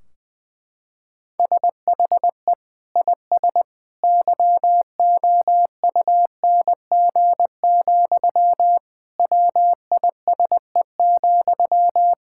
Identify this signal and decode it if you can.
Morse code, heard as a keyed tone, about 20 wpm, 700 Hz. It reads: SHE IS YOUNG, WISE,
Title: Sentences from Top 1600 Words 20wpm